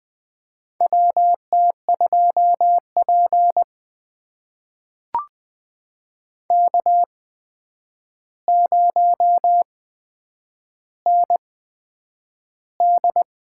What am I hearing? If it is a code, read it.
Morse code, heard as a keyed tone, about 20 wpm, spaced as for 9 wpm, 700 Hz. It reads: WT2P K0ND
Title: USA Call Signs - 4 Characters - 8x Character Spacing - 20wpm